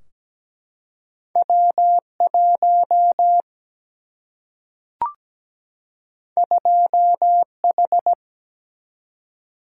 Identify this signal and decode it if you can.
Morse code, heard as a keyed tone, about 17 wpm, 700 Hz. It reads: W1 2H
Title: Single-Letter Single-Number 17wpm